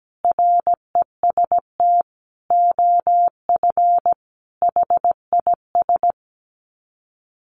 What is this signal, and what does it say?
Morse code, heard as a keyed tone, about 17 wpm, 700 Hz. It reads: REST OF HIS
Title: Sets of 5 Words 17wpm